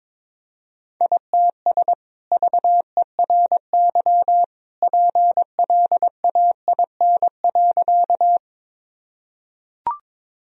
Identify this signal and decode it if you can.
Morse code, heard as a keyed tone, about 22 wpm, 700 Hz. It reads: ITS VERY PLAIN.
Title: Sentences from Top 1000 Words - Review Part 2 - 22wpm